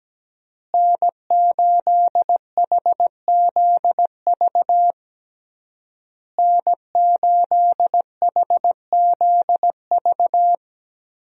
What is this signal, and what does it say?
Morse code, heard as a keyed tone, about 17 wpm, 700 Hz. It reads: N8HZV N8HZV
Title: USA Call Signs - 5 Characters - Repeated 3x - 17wpm